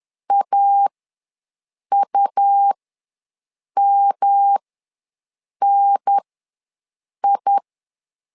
Morse code, heard as a keyed tone, about 10 wpm, spaced as for 6 wpm, 800 Hz. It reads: AUMNI
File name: LEZIONI / 3 MWRO / 3 M ws 10wpm 2121